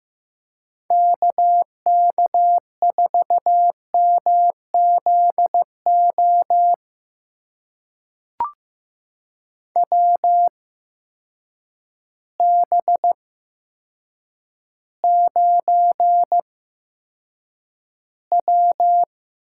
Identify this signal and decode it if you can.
Morse code, heard as a keyed tone, about 15 wpm, spaced as for 8 wpm, 700 Hz. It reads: KK4MZO WB9W
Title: USA Call Signs - 6 Characters - 8x Character Spacing - 15wpm